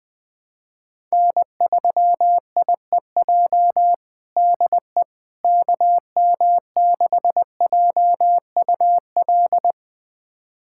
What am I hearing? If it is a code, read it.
Morse code, heard as a keyed tone, about 21 wpm, 700 Hz. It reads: N3IEJ DE KM6JUL
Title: USA Callsign DE Callsign 20wpm